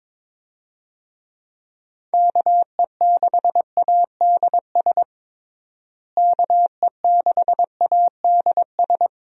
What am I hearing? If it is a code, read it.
Morse code, heard as a keyed tone, about 22 wpm, 700 Hz. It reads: KE6ADS KE6ADS
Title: USA Call Signs - 6 Characters - Repeated 2x - 22wpm